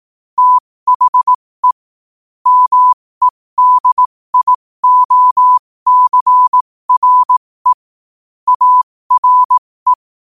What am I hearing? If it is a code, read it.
Morse code, heard as a keyed tone, about 18 wpm, spaced as for 15 wpm, 1000 Hz. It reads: THE MEDIOCRE ARE
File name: qotdfiles / 15wpm